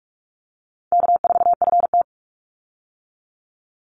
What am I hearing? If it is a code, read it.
Morse code, heard as a keyed tone, about 44 wpm, 700 Hz. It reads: K4FT
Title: Call Signs - 45wpm